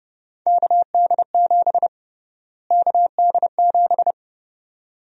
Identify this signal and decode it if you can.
Morse code, heard as a keyed tone, about 31 wpm, 700 Hz. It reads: KD7 KD7
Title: USA Call Sign Prefixes - Repeated 2x - 30wpm